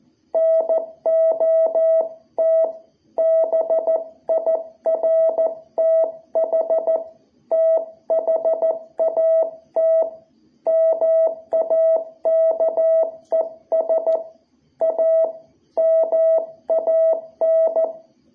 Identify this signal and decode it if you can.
Morse code, heard as a keyed tone, about 14 wpm, 660 Hz. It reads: NOT BIRTH THAT MAKES A MAN
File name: jh8cbh / 2021cw